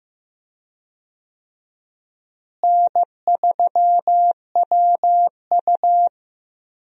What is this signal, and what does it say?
Morse code, heard as a keyed tone, about 15 wpm, 700 Hz. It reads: N3WU
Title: USA Call Signs - 4 Characters - Repeated 2x - 15wpm